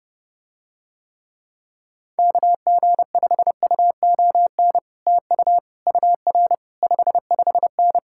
Morse code, heard as a keyed tone, about 30 wpm, 700 Hz. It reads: KG5UON TU UR 55N